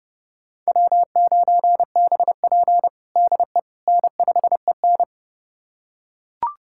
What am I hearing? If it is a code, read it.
Morse code, heard as a keyed tone, about 30 wpm, 700 Hz. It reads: W9BP DE N5EN
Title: USA Callsign DE Callsign - 4 Characters - 30wpm